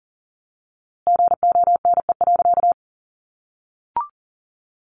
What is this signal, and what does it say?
Morse code, heard as a keyed tone, about 40 wpm, 700 Hz. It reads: GONE.
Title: A Sound Of Thunder 40wpm